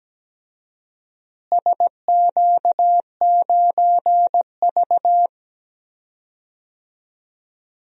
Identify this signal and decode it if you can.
Morse code, heard as a keyed tone, about 17 wpm, 700 Hz. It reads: SQ9V
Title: DXCC-100 Call Signs - 4 Characters - Repeated 2x - 17wpm